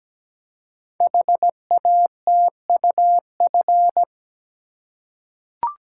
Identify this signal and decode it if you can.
Morse code, heard as a keyed tone, about 17 wpm, 700 Hz. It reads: HATUF
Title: Code Groups - 5 Letters - 17wpm